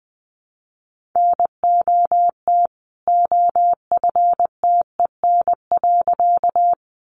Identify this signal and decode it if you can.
Morse code, heard as a keyed tone, about 20 wpm, 700 Hz. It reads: NOT OFTEN.
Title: A Sound Of Thunder 20wpm